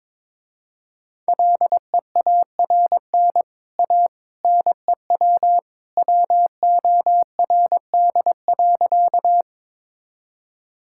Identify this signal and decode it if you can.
Morse code, heard as a keyed tone, about 22 wpm, 700 Hz. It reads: LEARN A NEW WORD.
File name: practice / 22wpm / Easy Sentences with Hints - 22wpm